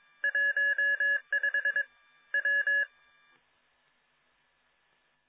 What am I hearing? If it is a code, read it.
Morse code, heard as a keyed tone, about 22 wpm, 1700 Hz. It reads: 15 W